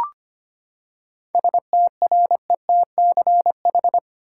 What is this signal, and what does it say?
Morse code, heard as a keyed tone, about 25 wpm, 700 Hz. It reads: STRETCH
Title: Seven Letter Words - 25wpm